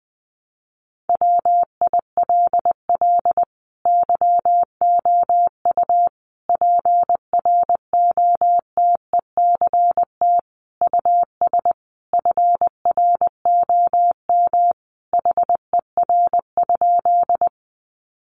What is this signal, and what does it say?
Morse code, heard as a keyed tone, about 20 wpm, 700 Hz. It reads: WILL YOU PROTECT US FROM HER?
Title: Sentences from Top 900 Words 20wpm